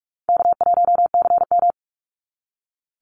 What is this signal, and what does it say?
Morse code, heard as a keyed tone, about 46 wpm, 690 Hz. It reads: K1CM
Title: NA SOTA Chaser Callsigns 45wpm